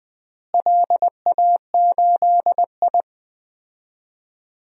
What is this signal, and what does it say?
Morse code, heard as a keyed tone, about 20 wpm, 700 Hz. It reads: LA8I
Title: DXCC-100 Call Signs - 4 Characters - Repeated 2x - 20wpm